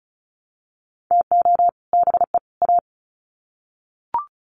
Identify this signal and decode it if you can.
Morse code, heard as a keyed tone, about 34 wpm, 700 Hz. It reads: TO BE A
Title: Sets of 3 Words 35wpm